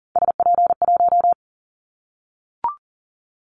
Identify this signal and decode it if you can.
Morse code, heard as a keyed tone, about 40 wpm, 700 Hz. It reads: SP1
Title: Callsign Prefixes for Top 25 DXCC - 40wpm